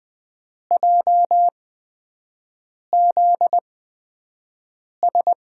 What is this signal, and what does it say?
Morse code, heard as a keyed tone, about 20 wpm, spaced as for 5 wpm, 700 Hz. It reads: JZS
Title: USA Call Signs - 5 Characters - 8x Character Spacing - 20wpm